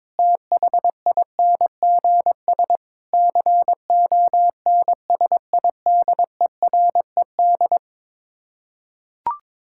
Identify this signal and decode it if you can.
Morse code, heard as a keyed tone, about 22 wpm, 700 Hz. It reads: THINGS CONSIDERED
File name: practice / 22wpm / Build A Phrase - 5-Word Phrases - 22wpm